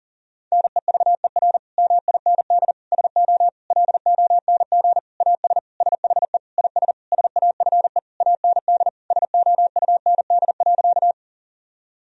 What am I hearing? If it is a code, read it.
Morse code, heard as a keyed tone, about 37 wpm, 700 Hz. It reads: NEVER MIND SO LONG AS SHE IS SAFE AND SOUND.
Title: Sentences from Top 1000 Words - Review Part 1 - 40wpm